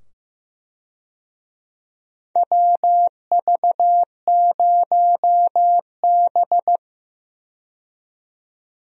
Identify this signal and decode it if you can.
Morse code, heard as a keyed tone, about 15 wpm, 700 Hz. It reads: WV0B
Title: USA Call Signs - 4 Characters - 15wpm